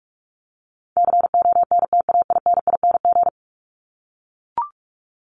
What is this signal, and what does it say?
Morse code, heard as a keyed tone, about 44 wpm, 700 Hz. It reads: CONTAINING
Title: ING Suffix - Top 100 Words with Base Words 45wpm